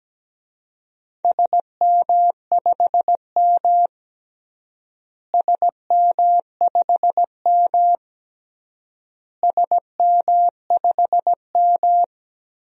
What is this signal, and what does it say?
Morse code, heard as a keyed tone, about 17 wpm, 700 Hz. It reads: SM5M SM5M SM5M
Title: DXCC-100 Call Signs - 4 Characters - Repeated 3x - 17wpm